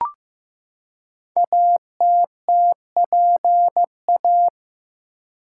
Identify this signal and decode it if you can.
Morse code, heard as a keyed tone, about 15 wpm, 700 Hz. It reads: ATTPA
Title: Code Groups - 5 Letters - 15wpm